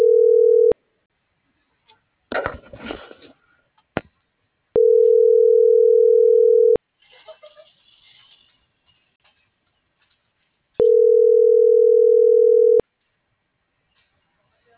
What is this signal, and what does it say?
Ringback tone (North America), 440 + 480 Hz.